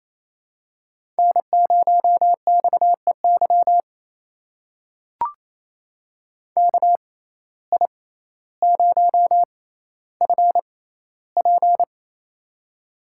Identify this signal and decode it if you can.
Morse code, heard as a keyed tone, about 27 wpm, spaced as for 14 wpm, 700 Hz. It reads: N0XEY KI0FP
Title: USA Call Signs - 5 Characters - 6x Character Spacing - 28wpm